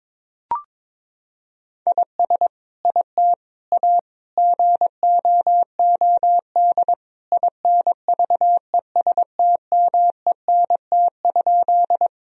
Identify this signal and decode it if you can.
Morse code, heard as a keyed tone, about 22 wpm, 700 Hz. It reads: IS IT A GOOD INVESTMENT?
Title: Sentences from Top 1000 Words - Review Part 4 - 22wpm